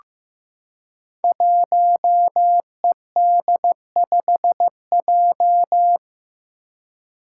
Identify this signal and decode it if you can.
Morse code, heard as a keyed tone, about 15 wpm, 700 Hz. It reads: 1ED5J
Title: Five Challenging Letter-Number Combinations 15wpm